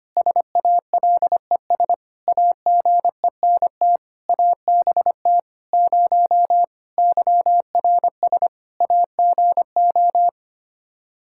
Morse code, heard as a keyed tone, about 25 wpm, 700 Hz. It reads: SALES AGENT ABT 0 YRS AGO
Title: Practice QSO - Exchange 3 - More Info - 25wpm